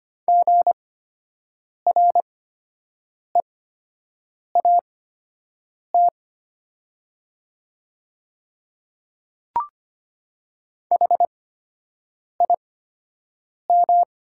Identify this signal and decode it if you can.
Morse code, heard as a keyed tone, about 25 wpm, spaced as for 7 wpm, 700 Hz. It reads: GREAT HIM
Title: 8x Character Spacing - Top 200 Words - 25wpm